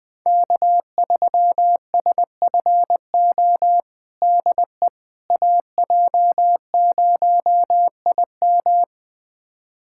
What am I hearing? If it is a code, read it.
Morse code, heard as a keyed tone, about 20 wpm, 700 Hz. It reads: K3SFO DE AJ0IM
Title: USA Callsign DE Callsign - 5 Characters - 20wpm